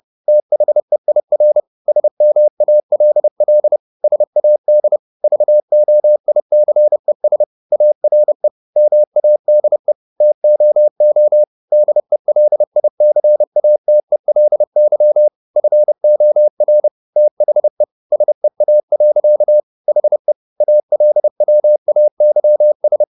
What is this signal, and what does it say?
Morse code, heard as a keyed tone, about 30 wpm, 600 Hz. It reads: THEIR SMALL SAD VOICES ARE MADE TOO DELICATELY FOR THE SEA. HE ALWAYS
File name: the old man and the sea / the old man and the sea 1 30wpm